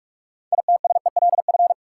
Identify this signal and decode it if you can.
Morse code, heard as a keyed tone, about 47 wpm, 700 Hz. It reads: ITSELF
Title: Six Letter Words - 45wpm